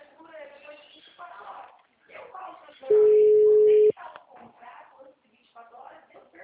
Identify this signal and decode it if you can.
Ringback tone (Europe, 425 Hz), with speech.
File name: out- -2001- - - .149